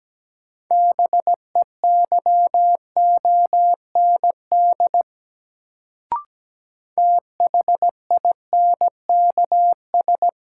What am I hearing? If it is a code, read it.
Morse code, heard as a keyed tone, about 17 wpm, 700 Hz. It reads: BEYOND THINKS
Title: Sentences from Top 1000 Words - Review Part 6 - New Word Drill - 17wpm